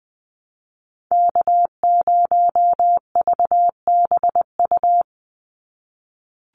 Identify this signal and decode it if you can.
Morse code, heard as a keyed tone, about 20 wpm, 700 Hz. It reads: K0VBU
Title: Call Signs - 20wpm